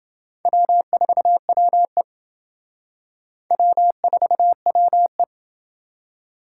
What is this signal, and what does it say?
Morse code, heard as a keyed tone, about 28 wpm, 700 Hz. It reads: W4WE W4WE